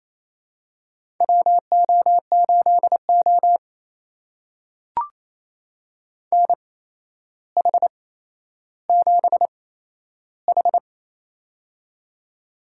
Morse code, heard as a keyed tone, about 28 wpm, spaced as for 12 wpm, 700 Hz. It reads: WO8O NH7H
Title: USA Call Signs - 4 Characters - 8x Character Spacing - 28wpm